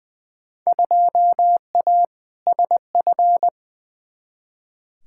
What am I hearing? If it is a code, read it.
Morse code, heard as a keyed tone, about 20 wpm, 700 Hz. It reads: 2A SF
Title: Field Day Exchanges - 20wpm